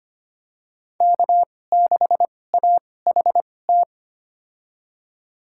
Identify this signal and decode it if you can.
Morse code, heard as a keyed tone, about 25 wpm, spaced as for 18 wpm, 700 Hz. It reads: K6AHT
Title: USA Call Signs - 5 Characters - 2x Character Spacing - 25wpm